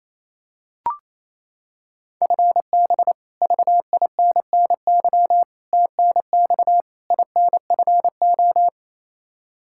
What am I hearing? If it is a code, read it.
Morse code, heard as a keyed tone, about 28 wpm, 700 Hz. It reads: FB VINNY TNX INFO
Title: Practice QSO - Exchange 3 - More Info - 28wpm